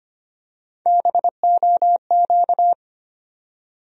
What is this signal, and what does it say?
Morse code, heard as a keyed tone, about 25 wpm, 700 Hz. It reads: BOQ
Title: USA Call Sign Postfixes - 3 Letters 25wpm